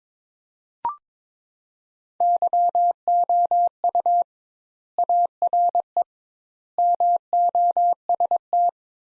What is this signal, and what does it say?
Morse code, heard as a keyed tone, about 22 wpm, 700 Hz. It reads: YOU ARE MOST
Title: Sentences Top 200 Words - 22wpm